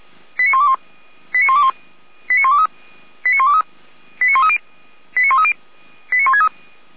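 Five-tone selective call (ZVEI), heard as quick runs of five tones, tones of about 70 ms. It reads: 89121 89121 89123 89123 89130 89130 89163
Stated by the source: ZVEI five-tone call 89022 received by SDR, on a server of Feuerwehr Orlamünde.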